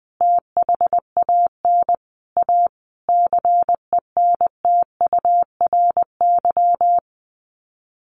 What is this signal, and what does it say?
Morse code, heard as a keyed tone, about 20 wpm, 700 Hz. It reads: THAN A CENTURY
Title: Sets of 5 Words Encore2 20wpm